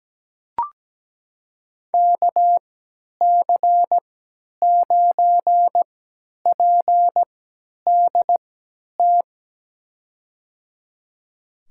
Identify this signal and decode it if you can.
Morse code, heard as a keyed tone, about 17 wpm, spaced as for 10 wpm, 700 Hz. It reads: KC9PDT
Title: USA Call Signs - 3x Character Spacing - 17wpm